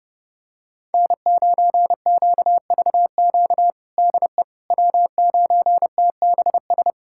Morse code, heard as a keyed tone, about 30 wpm, 700 Hz. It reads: N9QVQ DE W9TBS
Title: USA Callsign DE Callsign - 5 Characters - 30wpm